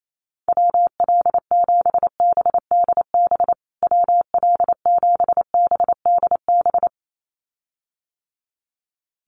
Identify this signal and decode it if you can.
Morse code, heard as a keyed tone, about 28 wpm, 700 Hz. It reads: WL7BDB WL7BDB
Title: USA Call Signs Repeated 2x 28wpm